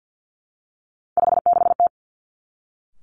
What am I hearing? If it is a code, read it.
Morse code, heard as a keyed tone, about 50 wpm, 700 Hz. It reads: 56T
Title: Three Challenging Letter-Number Combinations 50wpm